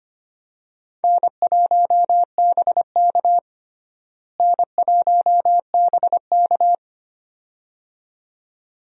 Morse code, heard as a keyed tone, about 25 wpm, 700 Hz. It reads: N1BK N1BK
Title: USA Call Signs - 4 Characters - Repeated 2x - 25wpm